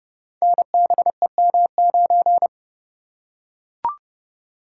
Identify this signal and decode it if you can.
Morse code, heard as a keyed tone, about 29 wpm, 700 Hz. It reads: NBEM9
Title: Code Groups - 5 Letter:Numbers - 30wpm